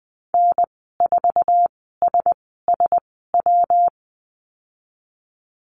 Morse code, heard as a keyed tone, about 20 wpm, spaced as for 15 wpm, 700 Hz. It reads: N4SSW
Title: 167 Lesson 27g - Number 7 - Call Signs 20wpm